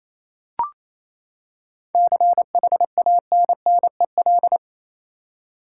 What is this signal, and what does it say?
Morse code, heard as a keyed tone, about 28 wpm, 700 Hz. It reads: CHANNEL
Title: Seven Letter Words - 28wpm